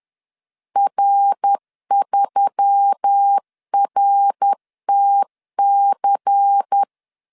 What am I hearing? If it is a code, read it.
Morse code, heard as a keyed tone, about 10 wpm, 800 Hz. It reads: R3RTC
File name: CW / DET / 1 9 1500C S A 10wpm 4511